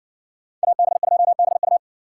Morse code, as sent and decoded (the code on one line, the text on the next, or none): .- -... ..--- -... ..-
AB2BU